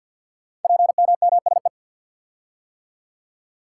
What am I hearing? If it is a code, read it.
Morse code, heard as a keyed tone, about 49 wpm, 700 Hz. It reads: PMMSE
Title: Code Groups - 5 Letters - 50wpm